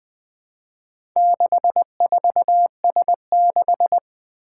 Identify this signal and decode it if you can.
Morse code, heard as a keyed tone, about 20 wpm, 700 Hz. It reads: 64S6
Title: Four Challenging Letter-Number Combinations 20wpm